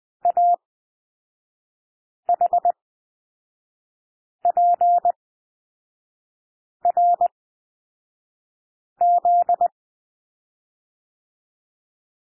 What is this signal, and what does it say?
Morse code, heard as a keyed tone, about 20 wpm, spaced as for 5 wpm, 700 Hz. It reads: AHPRZ
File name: SOUND LETTERS 20 wpm